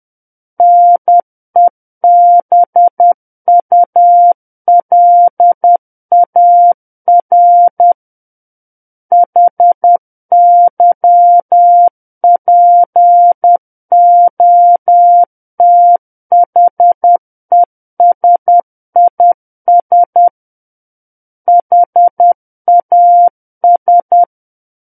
Morse code, heard as a keyed tone, about 10 wpm, 700 Hz. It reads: NEBULAR HYPOTHESIS HAS
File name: Book One / 01-Chapter 1 - 10 WPM